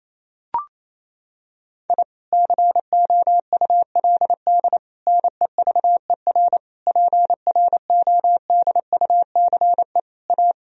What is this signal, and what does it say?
Morse code, heard as a keyed tone, about 28 wpm, 700 Hz. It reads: I COULD NEVER PRODUCE A
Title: Sentences from Top 1000 Words - Review Part 3 - 28wpm